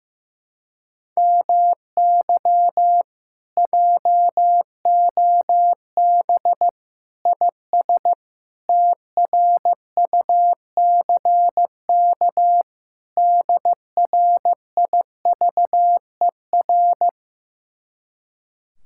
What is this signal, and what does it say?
Morse code, heard as a keyed tone, about 15 wpm, 700 Hz. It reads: MY JOB IS TRUCK DRIVER
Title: Speed-Racing Common QSO Phrases Encore - 15wpm